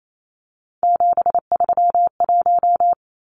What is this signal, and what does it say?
Morse code, heard as a keyed tone, about 28 wpm, 700 Hz. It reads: 731
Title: Numbers - 3-Digit 28wpm